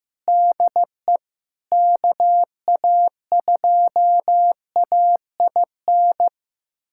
Morse code, heard as a keyed tone, about 16 wpm, 700 Hz. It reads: DE KA2AIN
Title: USA Callsign DE Callsign - 6 Characters - 15wpm